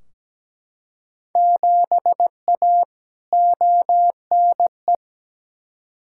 Morse code, heard as a keyed tone, about 17 wpm, 700 Hz. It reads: 7A ONE
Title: Field Day Exchanges Encore - 17wpm